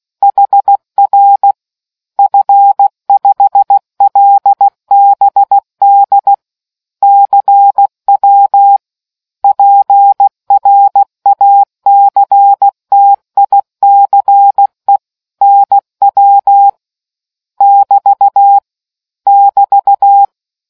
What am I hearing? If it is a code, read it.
Morse code, heard as a keyed tone, about 16 wpm, 800 Hz. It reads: HR F5LBD CW PRACTICE NW <BT> <BT>